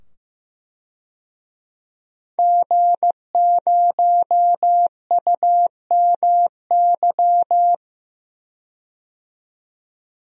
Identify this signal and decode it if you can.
Morse code, heard as a keyed tone, about 15 wpm, 700 Hz. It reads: G0UMY
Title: DXCC-100 Call Signs - 5 Characters - 15wpm